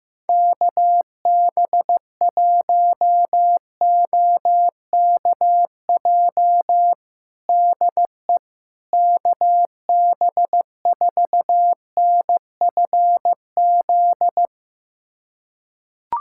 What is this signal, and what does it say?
Morse code, heard as a keyed tone, about 16 wpm, 700 Hz. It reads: KB1OKJ DE KB4NFZ
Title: USA Callsign DE Callsign 15wpm